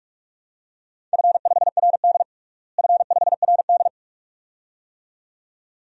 Morse code, heard as a keyed tone, about 46 wpm, 700 Hz. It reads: F5RD F5RD
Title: DXCC-100 Call Signs - 4 Characters - Repeated 2x - 45wpm